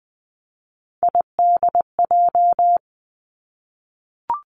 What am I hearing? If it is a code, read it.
Morse code, heard as a keyed tone, about 20 wpm, 700 Hz. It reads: IDJ
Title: Three Challenging Letter-Number Combinations 20wpm